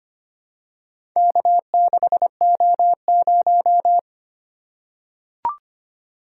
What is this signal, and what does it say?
Morse code, heard as a keyed tone, about 25 wpm, 700 Hz. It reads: K6O0
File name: practice / 25wpm / Four Letter-Number Combinations - 25wpm